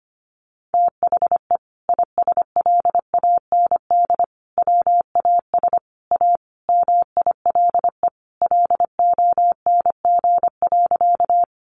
Morse code, heard as a keyed tone, about 25 wpm, 700 Hz. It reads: THE ISLAND WAS A MILE LONG.
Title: Sentences from Top 500 Words 25wpm